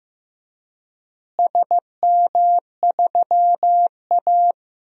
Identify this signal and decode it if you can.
Morse code, heard as a keyed tone, about 15 wpm, 700 Hz. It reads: SM3A
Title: DXCC-100 Call Signs - 4 Characters - Repeated 3x - 15wpm